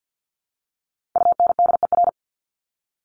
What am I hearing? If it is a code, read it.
Morse code, heard as a keyed tone, about 50 wpm, 700 Hz. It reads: UNDER
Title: Top 200 Words Repeated 50wpm